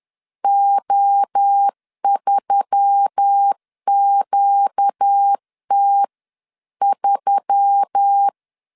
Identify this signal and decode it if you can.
Morse code, heard as a keyed tone, about 10 wpm, 800 Hz. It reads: O3QT3
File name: CW / DET / 1 6 1500C S B 10wpm 4137